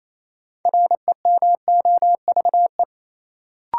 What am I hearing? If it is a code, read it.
Morse code, heard as a keyed tone, about 27 wpm, 700 Hz. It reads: REMOVE
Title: Sentences from Top 800 Words - New Word Drill - 28wpm